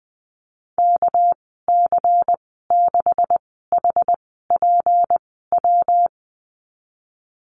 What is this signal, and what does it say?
Morse code, heard as a keyed tone, about 20 wpm, spaced as for 14 wpm, 700 Hz. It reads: KC6HPW
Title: 245 Lesson 39g - Letter z - Call Signs 20wpm